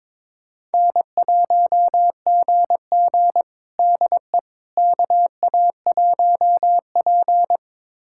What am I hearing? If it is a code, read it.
Morse code, heard as a keyed tone, about 22 wpm, 700 Hz. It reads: N1GG DE KA1P
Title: USA Callsign DE Callsign - 4 Characters - 22wpm